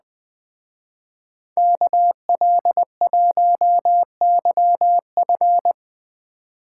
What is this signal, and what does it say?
Morse code, heard as a keyed tone, about 20 wpm, 700 Hz. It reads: KL1YF
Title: Speed Racing - USA Call Signs 17wpm